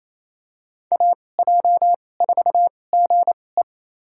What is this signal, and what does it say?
Morse code, heard as a keyed tone, about 28 wpm, spaced as for 20 wpm, 700 Hz. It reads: AJ4GE
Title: USA Call Signs - 2x Character Spacing - 28wpm